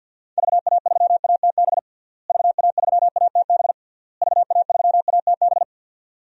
Morse code, heard as a keyed tone, about 52 wpm, 700 Hz. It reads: VA3ATB VA3ATB VA3ATB
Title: DXCC-100 Call Signs - 6 Characters - Repeated 3x - 50wpm